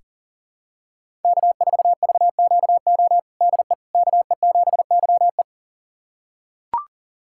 Morse code, heard as a keyed tone, about 37 wpm, 700 Hz. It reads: K4VQO DE KE7YE
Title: USA Callsign DE Callsign - 5 Characters - 40wpm